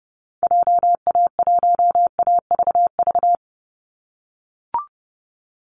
Morse code, heard as a keyed tone, about 30 wpm, 700 Hz. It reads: JA1AVV
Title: USA Call Signs 30wpm